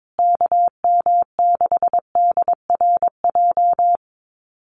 Morse code, heard as a keyed tone, about 22 wpm, 700 Hz. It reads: KM6DRJ
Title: USA Call Signs Repeated 3x 22wpm